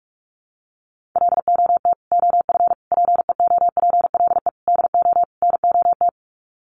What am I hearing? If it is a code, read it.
Morse code, heard as a keyed tone, about 44 wpm, 700 Hz. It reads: LOT OF PEOPLE DO NOT
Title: Sets of 5 Words Encore 45wpm